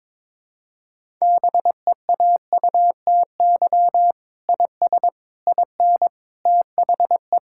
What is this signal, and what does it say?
Morse code, heard as a keyed tone, about 22 wpm, 700 Hz. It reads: BEAUTY IS IN THE
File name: Sentences with 7-Word Phrases - 22wpm